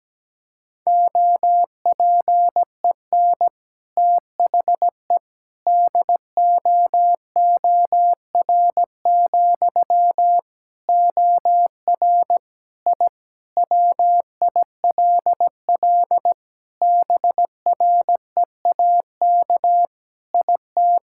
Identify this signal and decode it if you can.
Morse code, heard as a keyed tone, about 17 wpm, 700 Hz. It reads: OPEN THE DOOR, OR I WILL BREAK IT
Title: Sentences from Top 1000 Words - Review Part 6 - 17wpm